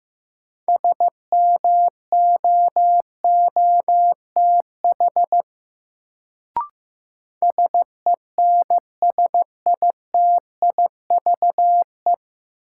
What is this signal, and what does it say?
Morse code, heard as a keyed tone, about 15 wpm, 700 Hz. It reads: SMOOTH SENSITIVE
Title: Sentences from Top 1400 Words - New Word Drill Alt - 15wpm